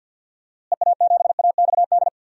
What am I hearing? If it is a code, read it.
Morse code, heard as a keyed tone, about 50 wpm, 700 Hz. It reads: EA7AXD
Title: DXCC-100 Call Signs - 6 Characters - Repeated 3x - 50wpm